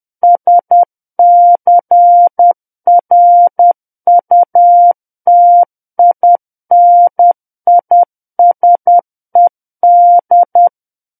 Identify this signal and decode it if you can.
Morse code, heard as a keyed tone, about 10 wpm, 700 Hz. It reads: SCRUTINISED
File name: Book One / 01-Chapter 1 - 10 WPM